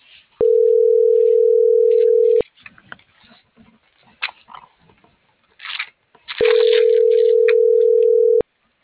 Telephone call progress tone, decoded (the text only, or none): ringback tone